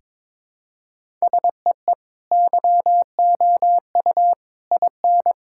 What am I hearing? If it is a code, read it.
Morse code, heard as a keyed tone, about 22 wpm, 700 Hz. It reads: SEE YOU IN
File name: practice / 22wpm / Sentences with 3-Word Phrases - 22wpm